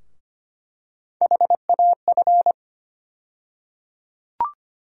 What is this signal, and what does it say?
Morse code, heard as a keyed tone, about 25 wpm, 700 Hz. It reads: HAF
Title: USA Call Sign Postfixes - 3 Letters 25wpm